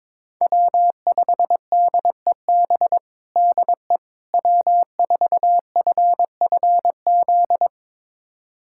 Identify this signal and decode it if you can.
Morse code, heard as a keyed tone, about 22 wpm, 700 Hz. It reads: W5DEB DE W4FFZ
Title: USA Callsign DE Callsign - 5 Characters - 22wpm